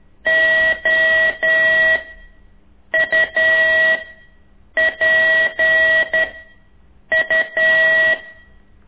Morse code, heard as a keyed tone, about 11 wpm, 1950 Hz. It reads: O U P U
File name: ik1weg2 / CORSI CW